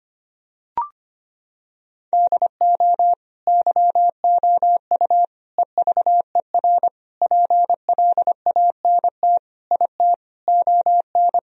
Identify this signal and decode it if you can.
Morse code, heard as a keyed tone, about 25 wpm, 700 Hz. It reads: DO YOU EVER PLANT IT ON
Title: Sentences from Top 1000 Words - Review Part 1 - 25wpm